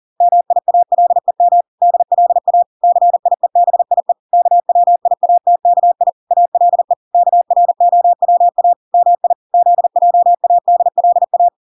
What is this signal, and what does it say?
Morse code, heard as a keyed tone, about 40 wpm, 700 Hz. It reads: MIALEM DLA CIEBIE KWIATKI ALE KROWA MI ZJADLA